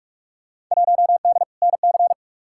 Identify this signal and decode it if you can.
Morse code, heard as a keyed tone, about 46 wpm, 700 Hz. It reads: 1D NC
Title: Field Day Exchanges - 45wpm